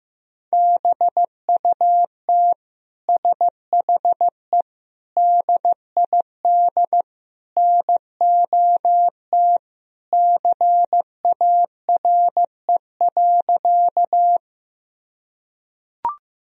Morse code, heard as a keyed tone, about 15 wpm, 700 Hz. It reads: BUT SHE DID NOT CARE.
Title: Sentences from Top 1000 Words - Review Part 1 - 15wpm